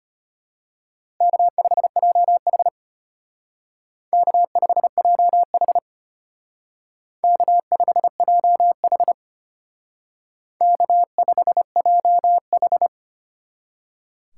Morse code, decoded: K5JH K5JH K5JH K5JH

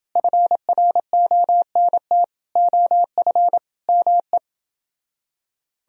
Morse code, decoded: FRONT OF ME

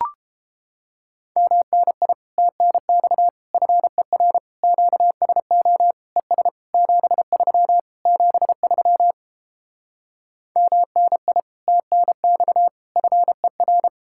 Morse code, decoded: MNI TNX FER QSO ES 73 73 MNI TNX FER

